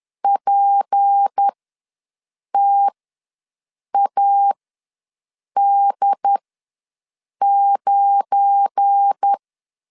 Morse code, heard as a keyed tone, about 10 wpm, spaced as for 6 wpm, 800 Hz. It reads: PTAD9
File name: LEZIONI / 10 SIGNS / 10D COMA ws 10wpm 4314